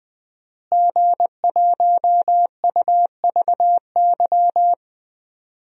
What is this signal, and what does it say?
Morse code, heard as a keyed tone, about 20 wpm, 700 Hz. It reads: G1UVY
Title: DXCC-100 Call Signs - 5 Characters - Repeated 2x - 20wpm